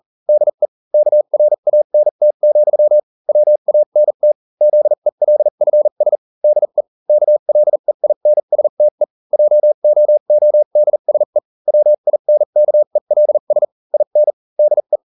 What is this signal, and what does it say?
Morse code, heard as a keyed tone, about 40 wpm, spaced as for 35 wpm, 600 Hz. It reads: DE KRANT, WANT ZELFS DE KLEINSTE JOODSE WINKELS IN DE